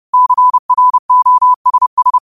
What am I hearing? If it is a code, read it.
Morse code, heard as a keyed tone, about 30 wpm, 1000 Hz. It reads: CROSS
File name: qotdfiles / 30wpm